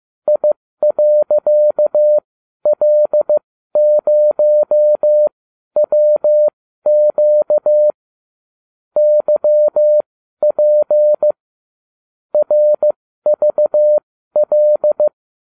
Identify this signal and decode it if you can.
Morse code, heard as a keyed tone, about 15 wpm, spaced as for 12 wpm, 600 Hz. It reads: I.L0WQ YP RVL